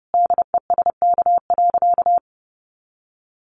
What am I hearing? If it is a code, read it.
Morse code, heard as a keyed tone, about 30 wpm, 700 Hz. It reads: DESK.
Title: Additional Easy Phrases 30wpm